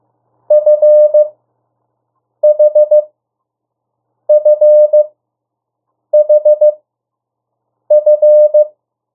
Morse code, heard as a keyed tone, about 15 wpm, spaced as for 6 wpm, 600 Hz. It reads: FHFHF
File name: morse mp3 / 02 urok 1